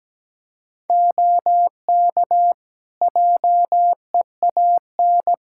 Morse code, heard as a keyed tone, about 17 wpm, 700 Hz. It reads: OK JEAN
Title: Practice QSO - Exchange 3 - More Info - 17wpm